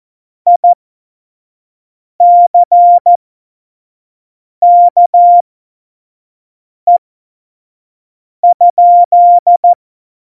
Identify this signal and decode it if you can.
Morse code, heard as a keyed tone, about 14 wpm, spaced as for 5 wpm, 700 Hz. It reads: ICKE?
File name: Morsekurs / Witze / Witz 19. September 17, Tempo 25